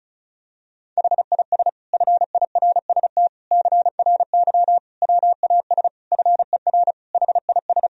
Morse code, heard as a keyed tone, about 34 wpm, 700 Hz. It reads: HIS FIRST CRY WAS FER HIS
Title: Sentences from Top 500 Words and Abbreviations 35wpm